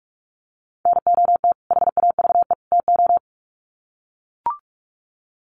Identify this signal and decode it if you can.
Morse code, heard as a keyed tone, about 44 wpm, 700 Hz. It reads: NOT HAVE TO
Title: Sets of 3 Words 45wpm